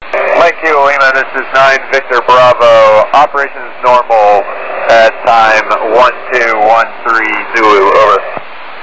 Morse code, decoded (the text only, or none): none